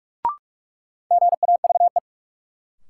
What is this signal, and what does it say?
Morse code, heard as a keyed tone, about 47 wpm, 700 Hz. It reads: GAVE
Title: Sentences from Top 500 Words - New Word Drill - 45wpm